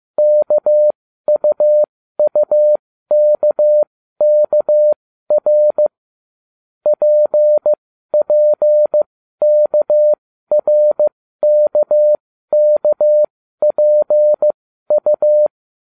Morse code, heard as a keyed tone, about 15 wpm, spaced as for 12 wpm, 600 Hz. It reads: KUUKKR PPKRKKPU